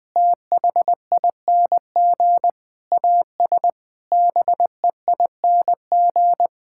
Morse code, heard as a keyed tone, about 20 wpm, 700 Hz. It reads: THING AS BEING
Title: Sentences with 6-Word Phrases - 20wpm